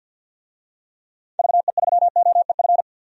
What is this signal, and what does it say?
Morse code, heard as a keyed tone, about 52 wpm, 700 Hz. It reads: VE3OEF
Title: DXCC-100 Call Signs - 6 Characters - 50wpm